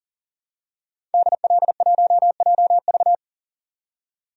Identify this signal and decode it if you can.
Morse code, heard as a keyed tone, about 40 wpm, 700 Hz. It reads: DL1JV
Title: DXCC-100 Call Signs - 5 Characters - 40wpm